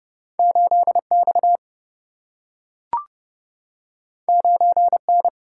Morse code, heard as a keyed tone, about 29 wpm, 700 Hz. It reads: 8 X 9 N
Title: Single-Letter Single-Number 30wpm